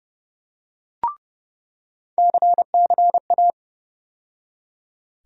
Morse code, heard as a keyed tone, about 30 wpm, 700 Hz. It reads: CCA
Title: USA Call Sign Postfixes - 3 Letters 30wpm